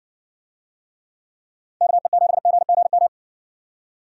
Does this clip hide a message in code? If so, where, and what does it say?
Yes: Morse code from 1.8 to 3.1 s, heard as a keyed tone, about 55 wpm, 700 Hz. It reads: KE7GGM